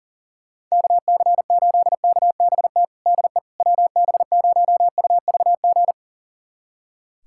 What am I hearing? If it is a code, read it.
Morse code, heard as a keyed tone, about 37 wpm, 700 Hz. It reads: KC8KBT DE WB0UVG
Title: USA Callsign DE Callsign - 6 Characters - 40wpm